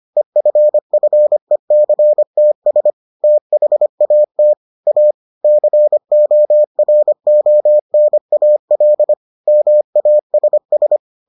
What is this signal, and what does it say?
Morse code, heard as a keyed tone, about 25 wpm, 600 Hz. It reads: EFFECTS THAT A CORONAL MASS